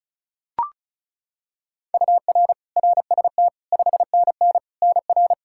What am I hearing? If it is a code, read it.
Morse code, heard as a keyed tone, about 34 wpm, 700 Hz. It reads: UR RST 5NN NR
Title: K1USN SST QSO Practice 35wpm